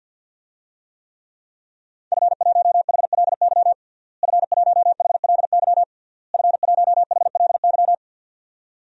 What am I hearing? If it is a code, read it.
Morse code, heard as a keyed tone, about 50 wpm, 700 Hz. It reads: F1HLY F1HLY F1HLY
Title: DXCC-100 Call Signs - 5 Characters - Repeated 3x - 50wpm